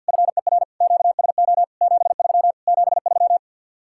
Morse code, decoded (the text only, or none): FER QSO 73 73